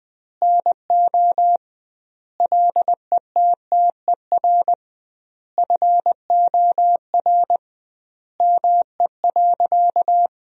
Morse code, heard as a keyed tone, about 20 wpm, 700 Hz. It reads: NO LETTER FOR ME.
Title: 2x Word-Spacing - Sentences from Top 300 Words - 20wpm